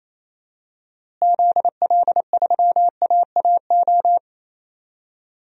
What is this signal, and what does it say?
Morse code, heard as a keyed tone, about 28 wpm, 700 Hz. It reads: ZL3AAO
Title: DXCC-100 Call Signs - 6 Characters - Repeated 2x - 28wpm